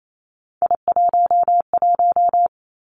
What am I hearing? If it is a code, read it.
Morse code, heard as a keyed tone, about 28 wpm, 700 Hz. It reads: I11
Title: Three Challenging Letter-Number Combinations 28wpm